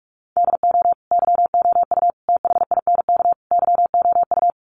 Morse code, heard as a keyed tone, about 44 wpm, 700 Hz. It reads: DO YOU THINK YOU
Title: Sets of 4 Words 45wpm